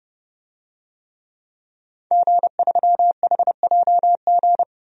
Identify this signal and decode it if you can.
Morse code, heard as a keyed tone, about 30 wpm, 700 Hz. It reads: G3HJG
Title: DXCC-100 Call Signs - 5 Characters - 30wpm